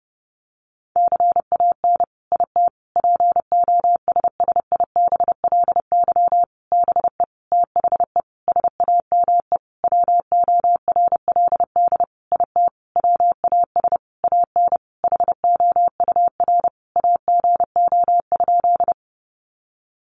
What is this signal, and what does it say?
Morse code, heard as a keyed tone, about 30 wpm, 700 Hz. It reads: CAN IT POSSIBLY BE THE SAME WORLD IT WAS AN HOUR AGO?
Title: Sentences from Top 1300 Words 30wpm